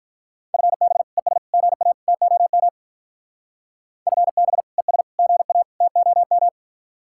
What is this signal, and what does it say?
Morse code, heard as a keyed tone, about 42 wpm, 700 Hz. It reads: FB ES GA TOM FB ES GA TOM
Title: Speed-Racing Common QSO Phrases Encore - 35wpm